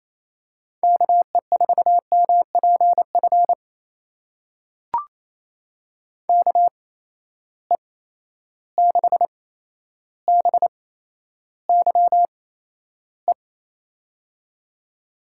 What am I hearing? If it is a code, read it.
Morse code, heard as a keyed tone, about 28 wpm, spaced as for 12 wpm, 700 Hz. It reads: KE4MPF KE6BYE
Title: USA Call Signs - 6 Characters - 8x Character Spacing - 28wpm